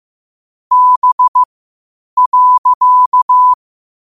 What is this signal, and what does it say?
Morse code, heard as a keyed tone, about 15 wpm, spaced as for 8 wpm, 1000 Hz. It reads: B.
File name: qotdfiles / 8wpm